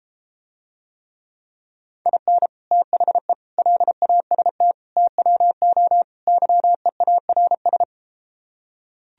Morse code, heard as a keyed tone, about 33 wpm, 700 Hz. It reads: IN THE LAST TWO YEARS